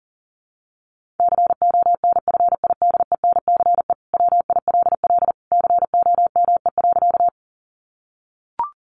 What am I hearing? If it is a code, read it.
Morse code, heard as a keyed tone, about 37 wpm, 700 Hz. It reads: CONFIDENCE WILL COME.
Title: Sentences from Top 1500 Words 40wpm